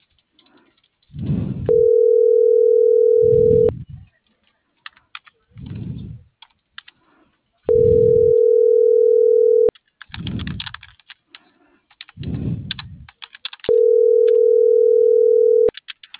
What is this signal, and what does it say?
Ringback tone (North America), 440 + 480 Hz.